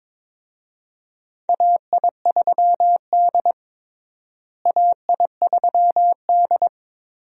Morse code, heard as a keyed tone, about 22 wpm, 700 Hz. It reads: AI3D AI3D
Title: USA Call Signs - 4 Characters - Repeated 3x - 22wpm